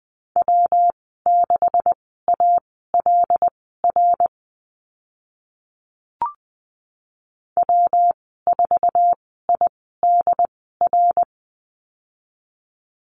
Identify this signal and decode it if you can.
Morse code, heard as a keyed tone, about 20 wpm, spaced as for 14 wpm, 700 Hz. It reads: W6ALR W4IDR
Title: 154 Lesson 25g - Letter p - Call Signs 20wpm